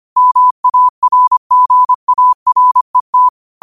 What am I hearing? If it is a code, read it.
Morse code, heard as a keyed tone, about 25 wpm, 1000 Hz. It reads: MARGARET